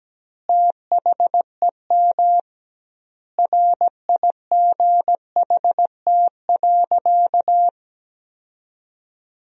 Morse code, heard as a keyed tone, about 17 wpm, 700 Hz. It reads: THEM RIGHT.
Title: 2x Word-Spacing - Sentences from Top 400 Words - 17wpm